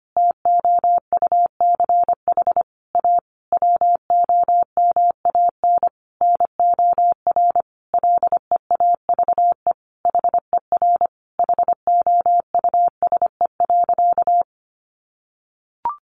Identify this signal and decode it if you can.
Morse code, heard as a keyed tone, about 25 wpm, 700 Hz. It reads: TOUCH A WOMAN NOR LEAVE HER HOUSE.
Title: Sentences from Top 1500 Words 25wpm